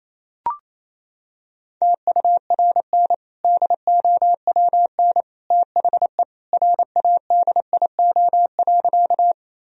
Morse code, heard as a keyed tone, about 28 wpm, 700 Hz. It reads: TURN DOWN THE RADIO.
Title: Sentences from Top 1000 Words - Review Part 3 - 28wpm